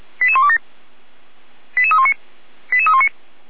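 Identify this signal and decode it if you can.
Five-tone selective call (ZVEI), heard as quick runs of five tones, tones of about 70 ms. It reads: 88127 88319 88319
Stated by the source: ZVEI five-tone selective call 88191 received via SDR.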